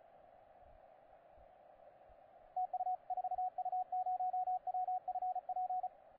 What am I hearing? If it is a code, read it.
Morse code, heard as a keyed tone, about 36 wpm, 750 Hz, with radio noise. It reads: TU 4U0WFP
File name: posts / 4u0wfp-15cw